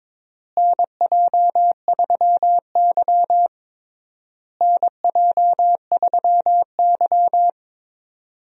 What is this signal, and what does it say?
Morse code, heard as a keyed tone, about 22 wpm, 700 Hz. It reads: NJ3Y NJ3Y
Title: USA Call Signs - 4 Characters - Repeated 3x - 22wpm